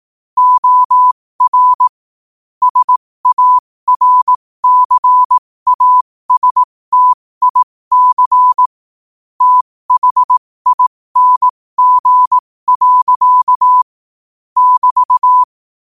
Morse code, heard as a keyed tone, about 18 wpm, spaced as for 15 wpm, 1000 Hz. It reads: OR SARCASTIC THING. <BT>
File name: qotdfiles / 15wpm